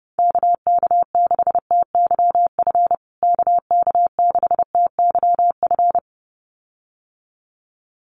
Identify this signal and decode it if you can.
Morse code, heard as a keyed tone, about 30 wpm, 700 Hz. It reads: KK6TYF KK6TYF
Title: USA Call Signs Repeated 2x 30wpm